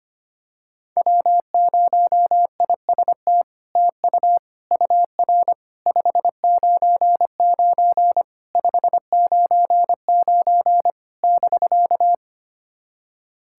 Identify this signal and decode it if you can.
Morse code, heard as a keyed tone, about 25 wpm, 700 Hz. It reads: W0IST TU UR 599 599 <BK>
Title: SOTA QSO Practice 25wpm